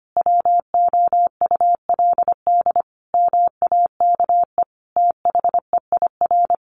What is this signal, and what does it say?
Morse code, heard as a keyed tone, about 25 wpm, 700 Hz. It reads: WOULD MAKE THEIR
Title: Sentences from Top 1000 Words 25wpm